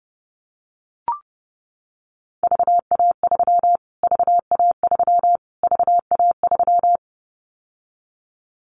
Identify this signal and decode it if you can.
Morse code, heard as a keyed tone, about 30 wpm, 700 Hz. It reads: VA3 VA3 VA3
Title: Callsign Prefixes for Top 25 DXCC - Repeated 3x - 30wpm